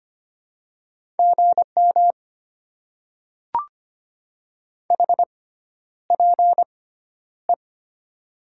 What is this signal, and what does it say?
Morse code, heard as a keyed tone, about 25 wpm, spaced as for 11 wpm, 700 Hz. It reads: GM HPE
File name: Top 90 QSO Elements - 6x Character Spacing - 25wpm